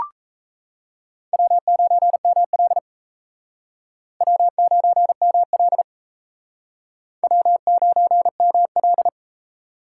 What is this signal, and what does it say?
Morse code, heard as a keyed tone, about 34 wpm, 700 Hz. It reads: W9ML W9ML W9ML
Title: Speed Racing - USA 1x2 Call Signs 28wpm